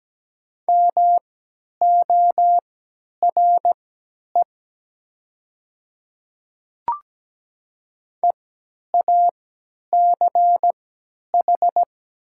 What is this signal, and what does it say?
Morse code, heard as a keyed tone, about 17 wpm, spaced as for 10 wpm, 700 Hz. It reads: MORE EACH
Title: 3x Character Spacing - Top 100 Words - 17wpm